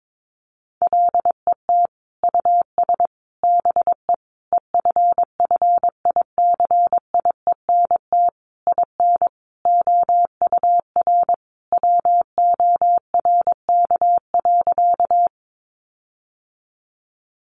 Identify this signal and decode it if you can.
Morse code, heard as a keyed tone, about 22 wpm, 700 Hz. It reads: LET US BE EFFICIENT IN OUR WORK.
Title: Sentences from Top 1300 Words 22wpm